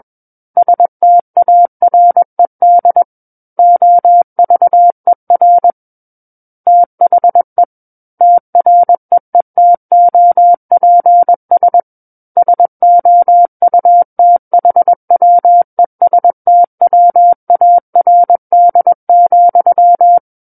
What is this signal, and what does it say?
Morse code, heard as a keyed tone, about 21 wpm, 700 Hz. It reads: STARED OVER THE TREETOPS SOUTHWESTWARD,
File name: Book One / 12-Chapter 12 - 21 WPM